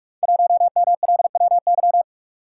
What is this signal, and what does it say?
Morse code, heard as a keyed tone, about 44 wpm, 690 Hz. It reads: 1MLWY
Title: Code Groups - 5 Letter:Numbers - 45wpm